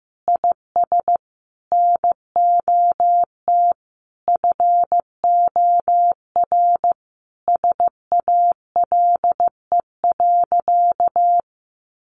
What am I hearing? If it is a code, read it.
Morse code, heard as a keyed tone, about 15 wpm, 700 Hz. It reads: IS NOT FOR SALE.
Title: Sentences from Top 1400 Words 15wpm